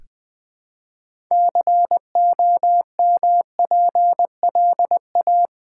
Morse code, heard as a keyed tone, about 20 wpm, 700 Hz. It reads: COMPLA
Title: Build A Word - 3 Letters At A Time - 8-Letter Words - 20wpm